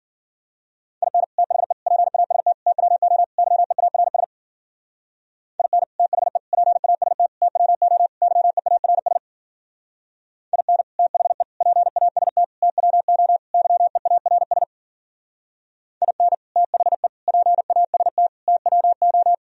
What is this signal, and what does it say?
Morse code, heard as a keyed tone, about 50 wpm, 700 Hz. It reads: IN THE PAST TWO YEARS IN THE PAST TWO YEARS IN THE PAST TWO YEARS IN THE PAST TWO